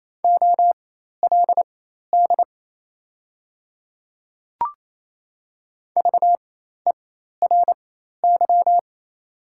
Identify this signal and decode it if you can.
Morse code, heard as a keyed tone, about 28 wpm, spaced as for 13 wpm, 700 Hz. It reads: OLD VERY
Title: 4x Character Spacing - Top 300 Words - 28wpm